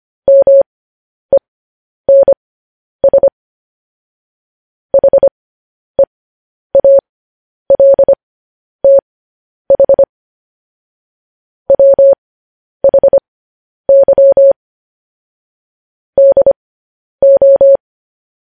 Morse code, heard as a keyed tone, about 25 wpm, spaced as for 10 wpm, 550 Hz. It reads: MENS HEALTH WHY DO
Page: News Headlines in Morse Code at 10 WPM